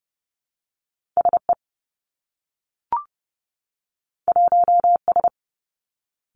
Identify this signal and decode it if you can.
Morse code, heard as a keyed tone, about 30 wpm, 700 Hz. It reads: S E 1 S
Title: Two Challenging Letter-Number Combinations 30wpm